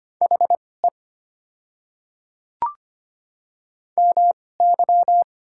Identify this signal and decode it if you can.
Morse code, heard as a keyed tone, about 25 wpm, spaced as for 18 wpm, 690 Hz. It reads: HE MY
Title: 2x Character Spacing - Top 300 Words - 25wpm